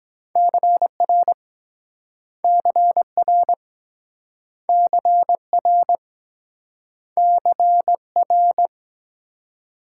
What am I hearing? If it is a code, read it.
Morse code, heard as a keyed tone, about 23 wpm, 700 Hz. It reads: CR CR CR CR